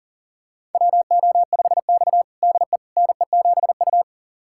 Morse code, heard as a keyed tone, about 40 wpm, 700 Hz. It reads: WO5X DE NE7U
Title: USA Callsign DE Callsign - 4 Characters - 40wpm